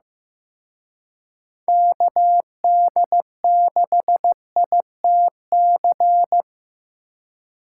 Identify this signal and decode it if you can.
Morse code, heard as a keyed tone, about 18 wpm, 700 Hz. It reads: KD6ITC
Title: Speed Racing - USA Call Signs 15wpm